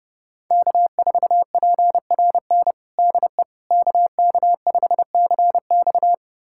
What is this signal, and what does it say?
Morse code, heard as a keyed tone, about 32 wpm, 700 Hz. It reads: K4PRN DE KK5CX
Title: USA Callsign DE Callsign - 5 Characters - 30wpm